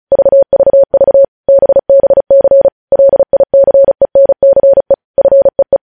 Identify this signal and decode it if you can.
Morse code, heard as a keyed tone, about 36 wpm, 550 Hz. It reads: VVV BBC LICENCE FEE
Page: News headlines in Morse code at 35 wpm